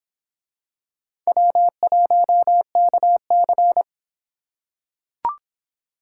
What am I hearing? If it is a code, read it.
Morse code, heard as a keyed tone, about 25 wpm, 700 Hz. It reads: W1KC